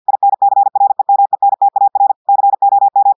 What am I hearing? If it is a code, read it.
Morse code, heard as a keyed tone, about 50 wpm, 800 Hz. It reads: INCREMENTAR COM
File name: portugais texte en clair 2 vitesse 50 mots minutes